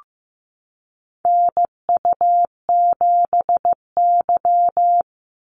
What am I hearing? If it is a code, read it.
Morse code, heard as a keyed tone, about 15 wpm, 700 Hz. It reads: NU7Y
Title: NA SOTA Activator Callsigns 15wpm